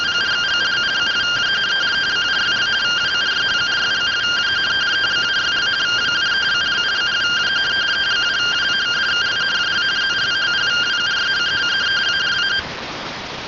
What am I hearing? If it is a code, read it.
Two-tone teleprinter signal, about 170 Hz shift.